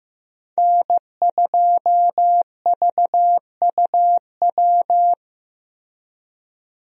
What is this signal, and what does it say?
Morse code, heard as a keyed tone, about 15 wpm, 700 Hz. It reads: N2VUW
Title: USA Call Signs - 5 Characters - Repeated 3x - 15wpm